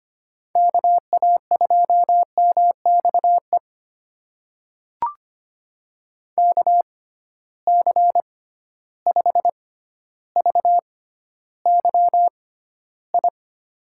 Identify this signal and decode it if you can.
Morse code, heard as a keyed tone, about 25 wpm, spaced as for 13 wpm, 700 Hz. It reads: KA2MXE KC5VYI
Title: USA Call Signs - 6 Characters - 6x Character Spacing - 25wpm